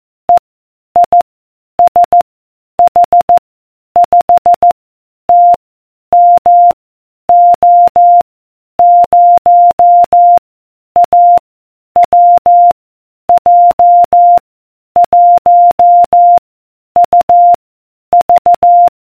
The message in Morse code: E I S H 5 T M O 0 A W J 1 U V